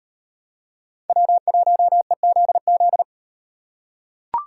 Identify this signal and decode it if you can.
Morse code, heard as a keyed tone, about 36 wpm, 700 Hz. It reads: W1EZZ